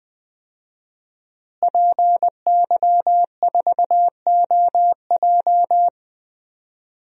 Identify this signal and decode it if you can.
Morse code, heard as a keyed tone, about 20 wpm, 700 Hz. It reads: PY4OJ
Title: DXCC-100 Call Signs - 5 Characters - Repeated 3x - 20wpm